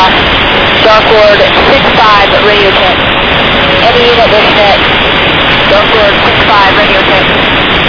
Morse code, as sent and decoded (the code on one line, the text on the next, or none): none